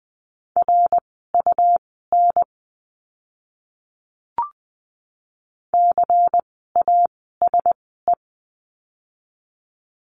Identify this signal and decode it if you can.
Morse code, heard as a keyed tone, about 20 wpm, spaced as for 14 wpm, 700 Hz. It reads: RUN CASE
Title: 237 Lesson 38f - Letter q - Words 20wpm